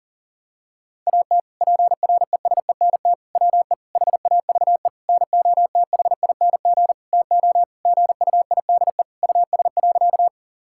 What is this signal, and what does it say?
Morse code, heard as a keyed tone, about 40 wpm, 700 Hz. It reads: AT PRESENT WE HAVE NOTHING TO GUIDE US.
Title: Sentences from Top 1000 Words - Review Part 5 - 40wpm